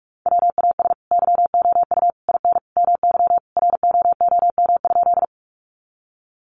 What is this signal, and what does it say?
Morse code, heard as a keyed tone, about 44 wpm, 700 Hz. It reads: WAS YOU IN MY ROOM?
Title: Sentences from Top 300 Words 45wpm